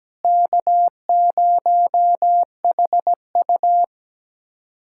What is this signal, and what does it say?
Morse code, heard as a keyed tone, about 17 wpm, 700 Hz. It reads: K0HU
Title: USA Callsign DE Callsign - 4 Characters - 17wpm